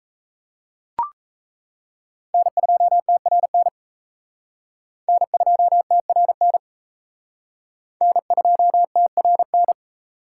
Morse code, heard as a keyed tone, about 35 wpm, 700 Hz. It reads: N2TRN N2TRN N2TRN
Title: Speed Racing - USA 1x3 Call Signs 28wpm